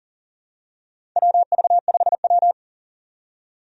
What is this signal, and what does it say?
Morse code, heard as a keyed tone, about 40 wpm, 700 Hz. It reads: WV5W
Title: USA Call Signs - 4 Characters - 40wpm